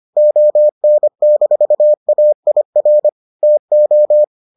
Morse code, heard as a keyed tone, about 25 wpm, 600 Hz. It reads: ON-AIR TO